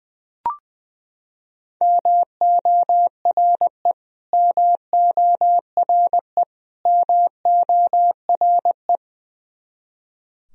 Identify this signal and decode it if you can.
Morse code, heard as a keyed tone, about 20 wpm, 700 Hz. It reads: MORE MORE MORE
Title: Top 100 Words Repeated 20wpm